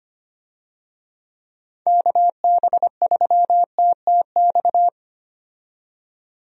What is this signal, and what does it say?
Morse code, heard as a keyed tone, about 25 wpm, 700 Hz. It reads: KB3TTX
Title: USA Call Signs - 6 Characters - 25wpm